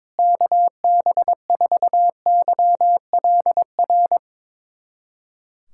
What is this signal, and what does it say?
Morse code, heard as a keyed tone, about 22 wpm, 700 Hz. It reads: KB4YLR
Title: USA Callsign DE Callsign - 6 Characters - 22wpm